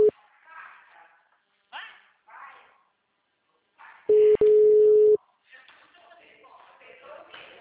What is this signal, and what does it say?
Ringback tone (Europe, 425 Hz), with speech.
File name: out- -2017- - - .59768